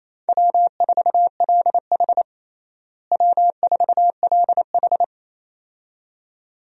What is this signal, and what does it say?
Morse code, heard as a keyed tone, about 28 wpm, 700 Hz. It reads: W4LH W4LH
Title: USA Call Signs - 4 Characters - Repeated 2x - 28wpm